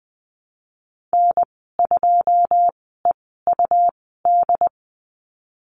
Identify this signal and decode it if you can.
Morse code, heard as a keyed tone, about 20 wpm, spaced as for 15 wpm, 700 Hz. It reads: N2EUD
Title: 154 Lesson 25g - Letter p - Call Signs 20wpm